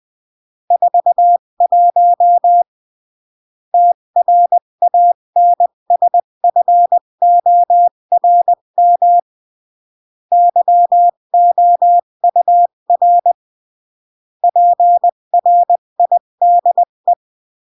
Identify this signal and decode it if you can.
Morse code, heard as a keyed tone, about 20 wpm, 700 Hz. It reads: 41 TRANSFORM YOUR PRIDE
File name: CW / jodorowsky / 82 commandments 20-15